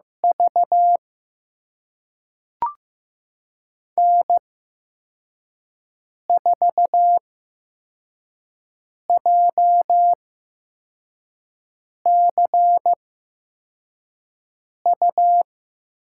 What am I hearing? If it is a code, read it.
Morse code, heard as a keyed tone, about 15 wpm, spaced as for 4 wpm, 700 Hz. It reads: V N4JCU
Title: USA Call Signs - 5 Characters - 8x Character Spacing - 15wpm